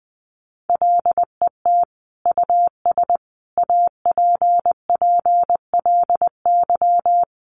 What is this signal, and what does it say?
Morse code, heard as a keyed tone, about 20 wpm, 700 Hz. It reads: LET US APPLY
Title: Sentences from Top 1200 Words 20wpm